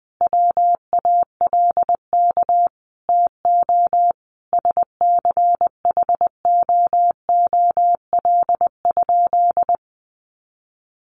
Morse code, heard as a keyed tone, about 20 wpm, 700 Hz. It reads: WALK TO SCHOOL?
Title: Sentences from Top 1400 Words 20wpm